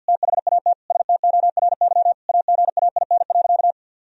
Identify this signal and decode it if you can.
Morse code, heard as a keyed tone, about 52 wpm, 700 Hz. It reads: THAT STORY AGAIN.
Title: Sentences from Top 1000 Words - Review Part 1 - 50wpm